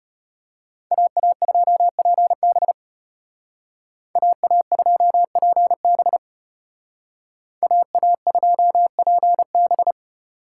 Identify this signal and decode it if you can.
Morse code, heard as a keyed tone, about 31 wpm, 700 Hz. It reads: AA2PB AA2PB AA2PB
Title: Speed Racing - USA 2x2 Call Signs 25wpm